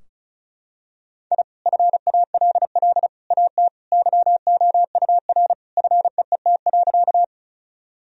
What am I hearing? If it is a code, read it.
Morse code, heard as a keyed tone, about 34 wpm, 700 Hz. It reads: I FALL AT YOUR FEET.
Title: Sentences from Top 1000 Words - Review Part 2 - 35wpm